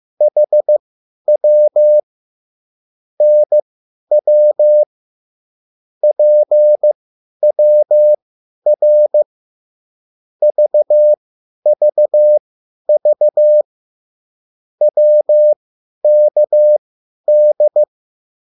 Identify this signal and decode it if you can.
Morse code, heard as a keyed tone, about 15 wpm, spaced as for 10 wpm, 600 Hz. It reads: HW NW PWR VVV WKD
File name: QSL / CW lesson